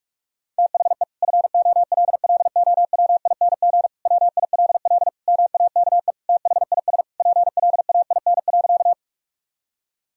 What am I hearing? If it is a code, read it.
Morse code, heard as a keyed tone, about 42 wpm, 700 Hz. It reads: THE FOLLOWING WILL MAKE THIS PLAIN.
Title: Sentences from Top 1000 Words - Review Part 4 - 45wpm